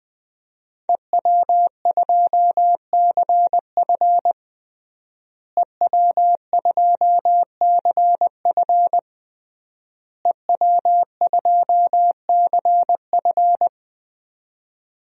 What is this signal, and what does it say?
Morse code, heard as a keyed tone, about 20 wpm, 700 Hz. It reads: EW2CF EW2CF EW2CF
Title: DXCC-100 Call Signs - 5 Characters - Repeated 3x - 20wpm